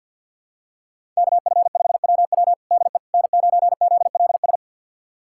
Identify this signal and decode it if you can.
Morse code, heard as a keyed tone, about 49 wpm, 700 Hz. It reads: KF5WW DE N9ZLS
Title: USA Callsign DE Callsign - 5 Characters - 50wpm